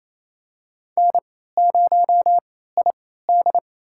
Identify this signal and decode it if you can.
Morse code, heard as a keyed tone, about 28 wpm, spaced as for 16 wpm, 700 Hz. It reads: N0ID
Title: USA Call Signs - 3x Character Spacing - 28wpm